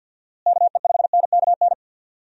Morse code, heard as a keyed tone, about 51 wpm, 700 Hz. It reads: KE5NKN
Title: USA Call Signs - 6 Characters - Repeated 3x - 50wpm